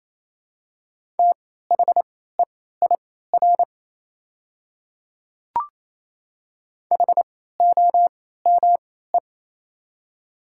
Morse code, heard as a keyed tone, about 27 wpm, spaced as for 16 wpm, 700 Hz. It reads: THEIR HOME
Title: 3x Character Spacing - Top 300 Words - 28wpm